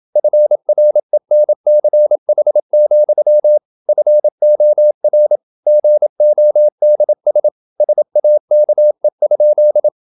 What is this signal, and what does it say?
Morse code, heard as a keyed tone, about 27 wpm, 600 Hz. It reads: FRENCH, FOR GODS SAKE?
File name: Never 0027wpm